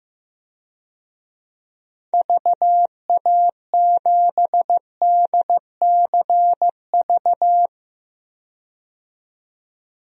Morse code, heard as a keyed tone, about 16 wpm, 700 Hz. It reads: VA7DCV
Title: DXCC-100 Call Signs - 6 Characters - 15wpm